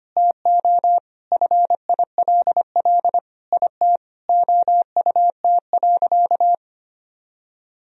Morse code, heard as a keyed tone, about 25 wpm, 700 Hz. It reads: TO FILL IT OUT.
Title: Sentences from Top 1000 Words - Review Part 3 - 25wpm